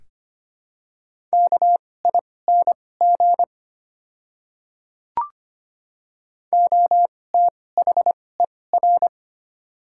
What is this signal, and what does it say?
Morse code, heard as a keyed tone, about 25 wpm, spaced as for 18 wpm, 700 Hz. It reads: KING OTHER
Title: 2x Character Spacing - Top 500 Words - 25wpm